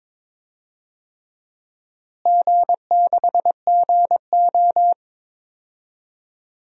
Morse code, heard as a keyed tone, about 22 wpm, 700 Hz. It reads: G6GO
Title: DXCC-100 Call Signs - 4 Characters - Repeated 3x - 22wpm